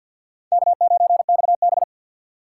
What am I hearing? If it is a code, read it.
Morse code, heard as a keyed tone, about 53 wpm, 700 Hz. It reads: K9XB
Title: USA Call Signs - 4 Characters - 3x Character Spacing - 50wpm